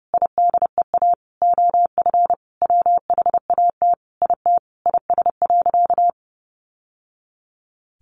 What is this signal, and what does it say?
Morse code, heard as a keyed tone, about 30 wpm, 700 Hz. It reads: IDEA OF WHAT IT IS.
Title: Sentences from Top 700 Words 30wpm